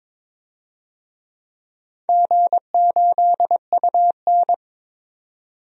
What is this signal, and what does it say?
Morse code, heard as a keyed tone, about 22 wpm, 700 Hz. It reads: G8UN
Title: DXCC-100 Call Signs - 4 Characters - 22wpm